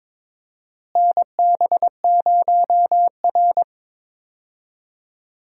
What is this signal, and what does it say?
Morse code, heard as a keyed tone, about 22 wpm, 700 Hz. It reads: NB0R
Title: USA Call Signs - 4 Characters - Repeated 2x - 22wpm